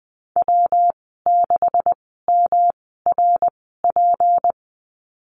Morse code, heard as a keyed tone, about 20 wpm, spaced as for 15 wpm, 700 Hz. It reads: W6MRP
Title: 173 Lesson 28g - Number 9 - Call Signs 20wpm